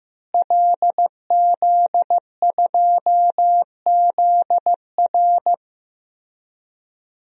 Morse code, heard as a keyed tone, about 15 wpm, 700 Hz. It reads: LZ2ZR
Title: DXCC-100 Call Signs - 5 Characters - Repeated 3x - 15wpm